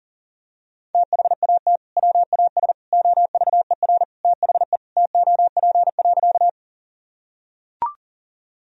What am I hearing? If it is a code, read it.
Morse code, heard as a keyed tone, about 40 wpm, 700 Hz. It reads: THAT WAS OVER THE TOP.
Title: Sentences from Top 1000 Words - Review Part 2 - 40wpm